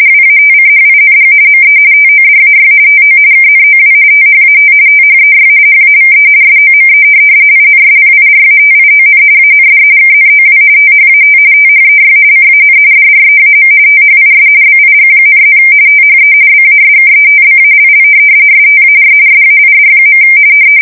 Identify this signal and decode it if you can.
RTTY, about 45.45 baud. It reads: WATCHING ME TODAY TO KNOW THAT THEY HAVE SOME INSPIRATION AND KNOW THAT THEY CAN DREAM THIS BIG,+ THE BASKETBALL STAR SAID.
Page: 170 Shift 2125/2295